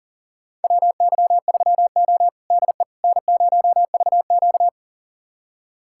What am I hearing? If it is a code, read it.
Morse code, heard as a keyed tone, about 37 wpm, 700 Hz. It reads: WY3O DE N0VQ
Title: USA Callsign DE Callsign - 4 Characters - 40wpm